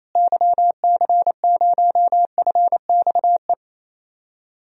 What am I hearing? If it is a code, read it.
Morse code, heard as a keyed tone, about 28 wpm, 700 Hz. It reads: YC0FXE
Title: DXCC-100 Call Signs - 6 Characters - Repeated 2x - 28wpm